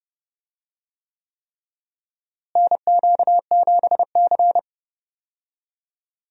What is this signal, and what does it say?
Morse code, heard as a keyed tone, about 32 wpm, 700 Hz. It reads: NQ7C
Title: USA Call Signs - 4 Characters - Repeated 3x - 30wpm